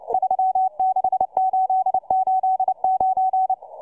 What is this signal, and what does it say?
Morse code, heard as a keyed tone, about 29 wpm, 750 Hz, with radio noise. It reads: 36889